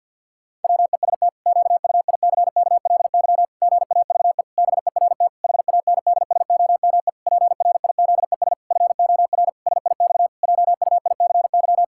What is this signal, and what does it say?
Morse code, heard as a keyed tone, about 45 wpm, 700 Hz. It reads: WEST QUICKLY GAVE BERT HANDSOME PRIZES FOR SIX JUICY
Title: Pangram Challenge v2 - 50wpm